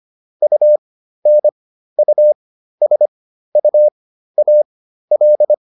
Morse code, heard as a keyed tone, about 25 wpm, spaced as for 13 wpm, 600 Hz. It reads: UNUSUAL